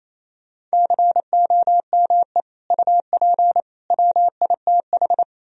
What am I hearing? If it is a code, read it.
Morse code, heard as a keyed tone, about 35 wpm, 700 Hz. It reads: COME UP WITH
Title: Speed-Racing - Sets of 3 Words 28wpm